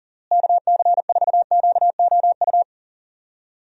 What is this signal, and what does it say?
Morse code, heard as a keyed tone, about 37 wpm, 700 Hz. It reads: KC4QOU
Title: USA Call Signs - 6 Characters - Repeated 2x - 40wpm